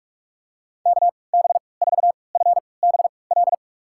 Morse code, heard as a keyed tone, about 46 wpm, spaced as for 25 wpm, 700 Hz. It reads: KB4FBL
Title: USA Call Signs - 3x Character Spacing - 45wpm